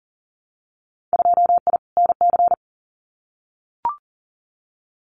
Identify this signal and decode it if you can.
Morse code, heard as a keyed tone, about 38 wpm, 700 Hz. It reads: 2I NC
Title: Winter Field Day Exchanges 40wpm